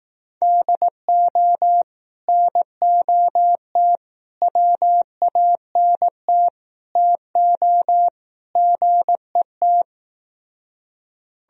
Morse code, heard as a keyed tone, about 18 wpm, 700 Hz. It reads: DO NOT WANT TO GET
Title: Speed-Racing - Sets of 5 Words 15wpm